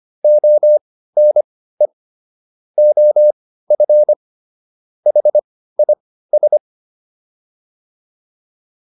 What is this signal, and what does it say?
Morse code, heard as a keyed tone, about 25 wpm, spaced as for 15 wpm, 600 Hz. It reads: ONE OF HIS
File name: WD205 15